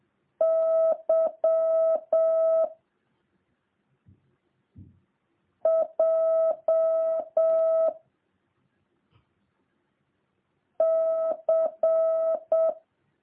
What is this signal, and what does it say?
Morse code, heard as a keyed tone, about 7 wpm, 650 Hz. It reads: Y J C